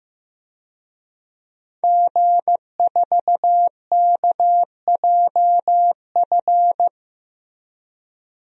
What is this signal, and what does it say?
Morse code, heard as a keyed tone, about 15 wpm, 700 Hz. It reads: G4KJF
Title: DXCC-100 Call Signs - 5 Characters - 15wpm